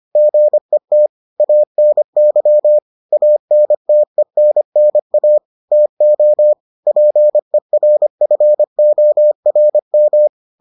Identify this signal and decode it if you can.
Morse code, heard as a keyed tone, about 25 wpm, 600 Hz. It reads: GET ANY ANTENNA TO PERFORM